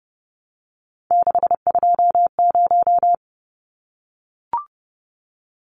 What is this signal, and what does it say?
Morse code, heard as a keyed tone, about 30 wpm, 700 Hz. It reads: 620
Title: Numbers - 3-Digit 30wpm